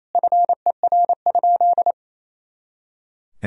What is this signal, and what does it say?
Morse code, heard as a keyed tone, about 28 wpm, 700 Hz. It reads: FER?
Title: Sentences from Top 100 Words and Abbreviations 28wpm